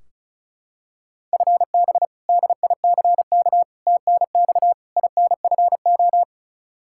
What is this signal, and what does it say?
Morse code, heard as a keyed tone, about 36 wpm, 700 Hz. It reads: FB DICK TNX INFO
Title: Practice QSO - Exchange 3 - More Info - 35wpm